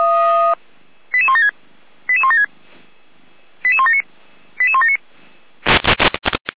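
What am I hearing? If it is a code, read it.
Five-tone selective call (ZVEI), heard as quick runs of five tones, tones of about 70 ms. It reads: 88176 88176 88179 88179
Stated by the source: ZVEI five-tone selective call received on an SDR.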